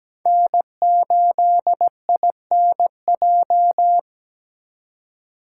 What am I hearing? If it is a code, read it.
Morse code, heard as a keyed tone, about 20 wpm, 700 Hz. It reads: N8INJ
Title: Speed Racing - USA Call Signs 17wpm